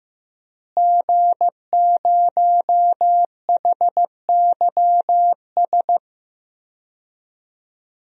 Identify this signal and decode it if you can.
Morse code, heard as a keyed tone, about 15 wpm, 700 Hz. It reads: G0HYS
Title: DXCC-100 Call Signs - 5 Characters - Repeated 3x - 15wpm